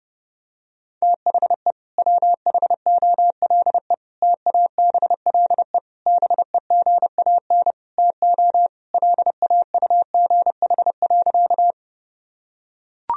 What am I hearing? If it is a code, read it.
Morse code, heard as a keyed tone, about 30 wpm, 700 Hz. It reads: THE WHOLE TABLE BEGAN TO LAUGH.
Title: Sentences from Top 1000 Words - Review Part 2 - 30wpm